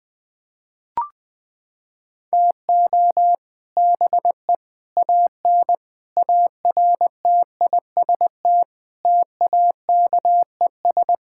Morse code, heard as a keyed tone, about 20 wpm, 700 Hz. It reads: TO BE AN ARTIST TAKES
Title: Sentences with 3-Word Phrases - 20wpm